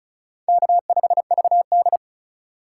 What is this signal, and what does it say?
Morse code, heard as a keyed tone, about 36 wpm, 700 Hz. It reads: K5VD
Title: USA Call Signs - 4 Characters - 4x Character Spacing - 35wpm